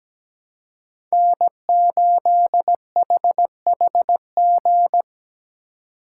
Morse code, heard as a keyed tone, about 17 wpm, 700 Hz. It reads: N8HHG
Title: K1USN SST QSO Practice - 17wpm